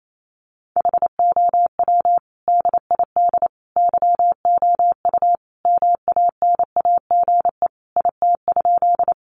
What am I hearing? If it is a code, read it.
Morse code, heard as a keyed tone, about 28 wpm, 700 Hz. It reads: HOW DID YOU MANAGE IT?
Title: Sentences from Top 1400 Words 28wpm